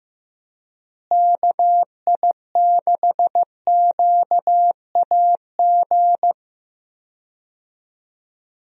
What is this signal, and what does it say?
Morse code, heard as a keyed tone, about 15 wpm, 700 Hz. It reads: KI6QAG
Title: USA Call Signs - 6 Characters - Repeated 3x - 15wpm